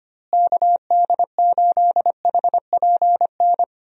Morse code, heard as a keyed tone, about 25 wpm, 700 Hz. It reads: KD8HPN
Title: USA Call Signs - 6 Characters - Repeated 3x - 25wpm